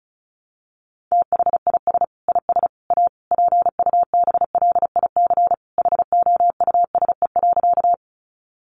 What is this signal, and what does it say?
Morse code, heard as a keyed tone, about 34 wpm, 700 Hz. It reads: THIS IS A PUBLIC HOUSE.
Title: Sentences from Top 600 Words 35wpm